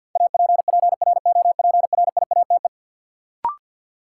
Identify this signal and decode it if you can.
Morse code, heard as a keyed tone, about 47 wpm, 700 Hz. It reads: APPROPRIATE
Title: Sentences from Top 900 Words - New Word Drill - 50wpm